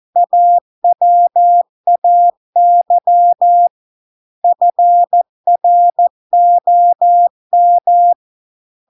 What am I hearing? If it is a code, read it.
Morse code, heard as a keyed tone, about 14 wpm, 700 Hz. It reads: AWAY FROM